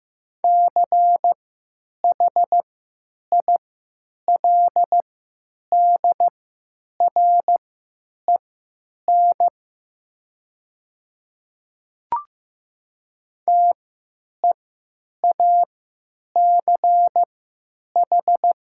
Morse code, heard as a keyed tone, about 15 wpm, spaced as for 9 wpm, 700 Hz. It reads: CHILDREN TEACH
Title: 3x Character Spacing - Top 500 Words - 15wpm